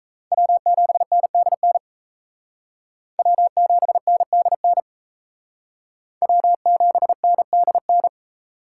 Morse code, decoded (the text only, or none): W7NDN W7NDN W7NDN